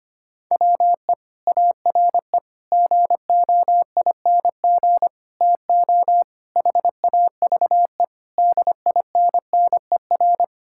Morse code, heard as a keyed tone, about 25 wpm, 700 Hz. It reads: WE ARE GOING TO HAVE DINNER
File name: practice / 25wpm / Sentences with 3-Word Phrases - 25wpm